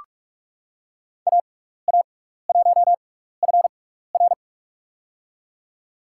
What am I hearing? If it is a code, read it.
Morse code, heard as a keyed tone, about 44 wpm, spaced as for 15 wpm, 700 Hz. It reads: AA1FR
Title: USA Call Signs - 5 Characters - 6x Character Spacing - 45wpm